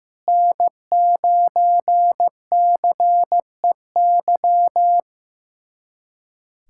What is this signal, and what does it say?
Morse code, heard as a keyed tone, about 15 wpm, 700 Hz. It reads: N9CEY
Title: USA Callsign DE Callsign - 5 Characters - 15wpm